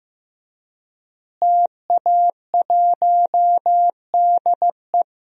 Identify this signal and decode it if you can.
Morse code, heard as a keyed tone, about 15 wpm, 700 Hz. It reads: TA1DE
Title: DXCC-100 Call Signs - 5 Characters - Repeated 2x - 15wpm